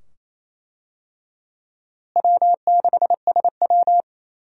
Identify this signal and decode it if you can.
Morse code, heard as a keyed tone, about 28 wpm, 700 Hz. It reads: W6SW
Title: USA Call Signs - 4 Characters - Repeated 2x - 28wpm